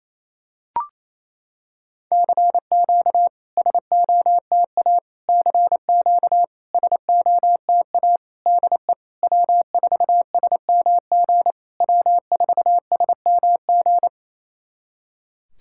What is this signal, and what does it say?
Morse code, heard as a keyed tone, about 28 wpm, 700 Hz. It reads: CQ SOTA CQ SOTA DE W4SMG W4SMG
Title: SOTA QSO Practice 28wpm